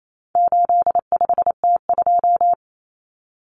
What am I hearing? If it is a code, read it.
Morse code, heard as a keyed tone, about 28 wpm, 700 Hz. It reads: 85T2
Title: Common Cut Numbers - 4-Digit 28wpm